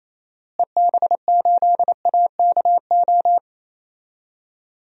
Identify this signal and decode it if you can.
Morse code, heard as a keyed tone, about 27 wpm, 700 Hz. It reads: EB8AKO
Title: DXCC-100 Call Signs - 6 Characters - Repeated 3x - 28wpm